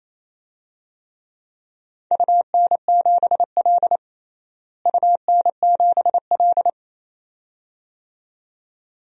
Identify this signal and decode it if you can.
Morse code, heard as a keyed tone, about 28 wpm, 700 Hz. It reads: UN7L UN7L
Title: DXCC-100 Call Signs - 4 Characters - Repeated 2x - 28wpm